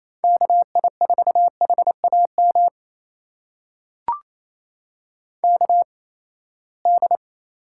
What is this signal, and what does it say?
Morse code, heard as a keyed tone, about 28 wpm, spaced as for 19 wpm, 700 Hz. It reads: KI4HAM KD
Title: USA Call Signs - 6 Characters - 8x Character Spacing - 28wpm